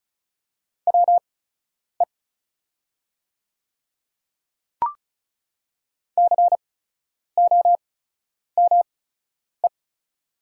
Morse code, heard as a keyed tone, about 36 wpm, spaced as for 10 wpm, 700 Hz. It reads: WE COME
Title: 8x Character Spacing - Top 100 Words - 35wpm